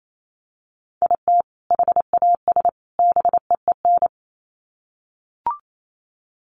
Morse code, heard as a keyed tone, about 28 wpm, 700 Hz. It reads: IT HAS BEEN
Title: Sets of 3 Words 28wpm